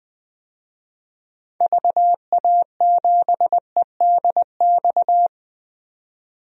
Morse code, heard as a keyed tone, about 20 wpm, 700 Hz. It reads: VA7EDX
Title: DXCC-100 Call Signs - 6 Characters - 20wpm